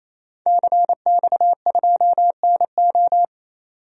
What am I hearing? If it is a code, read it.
Morse code, heard as a keyed tone, about 28 wpm, 700 Hz. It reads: CX2NO
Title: DXCC-100 Call Signs - 5 Characters - 28wpm